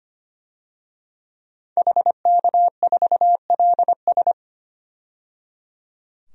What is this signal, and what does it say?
Morse code, heard as a keyed tone, about 25 wpm, 700 Hz. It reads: HK4LS
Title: DXCC-100 Call Signs - 5 Characters - 25wpm